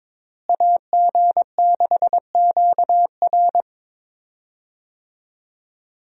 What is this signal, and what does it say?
Morse code, heard as a keyed tone, about 22 wpm, 700 Hz. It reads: AG6QR
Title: USA Call Signs - 5 Characters - Repeated 3x - 22wpm